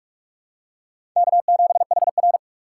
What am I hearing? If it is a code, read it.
Morse code, heard as a keyed tone, about 46 wpm, 700 Hz. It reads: K7HR
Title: USA Call Signs - 4x Character Spacing - 45wpm